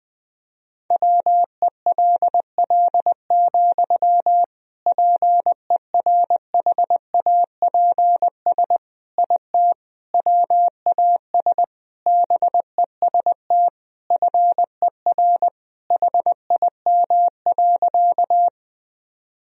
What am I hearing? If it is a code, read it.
Morse code, heard as a keyed tone, about 20 wpm, 700 Hz. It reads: WELL, PERHAPS IT WAS BEST FER HIM.
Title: Sentences from Top 500 Words and Abbreviations 20wpm